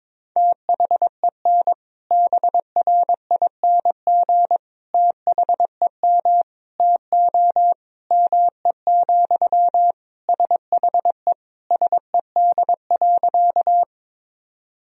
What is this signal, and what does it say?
Morse code, heard as a keyed tone, about 21 wpm, 700 Hz. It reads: THEN BRING THEM TO ME, SHE SED.
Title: Sentences from Top 500 Words and Abbreviations 22wpm